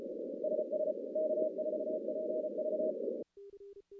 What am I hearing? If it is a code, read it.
Morse code, heard as a keyed tone, about 36 wpm, 620 Hz, with noise. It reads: SS K4PV
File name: contests / CW-SS-k5tr- -2200